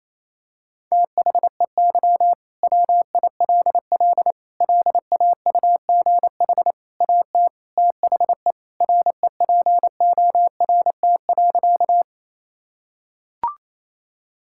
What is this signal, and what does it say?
Morse code, heard as a keyed tone, about 28 wpm, 700 Hz. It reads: THEY WILL LAUGH AT THE REPORT.
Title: Sentences from Top 1000 Words - Review Part 6 - 28wpm